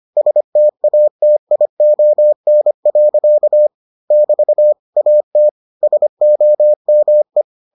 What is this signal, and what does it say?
Morse code, heard as a keyed tone, about 25 wpm, 600 Hz. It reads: STATION. <BT> AT SOME